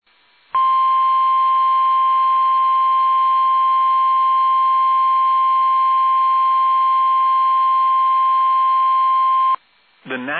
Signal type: Emergency Alert System (SAME) tones